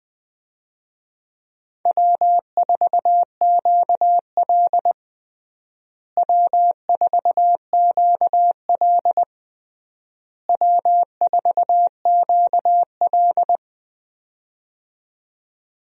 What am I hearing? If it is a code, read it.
Morse code, heard as a keyed tone, about 20 wpm, 700 Hz. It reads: W4QL W4QL W4QL
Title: USA Call Signs - 4 Characters - Repeated 3x - 20wpm